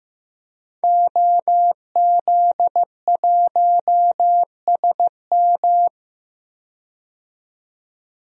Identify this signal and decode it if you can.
Morse code, heard as a keyed tone, about 15 wpm, 700 Hz. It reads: OZ1SM
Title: DXCC-100 Call Signs - 5 Characters - 15wpm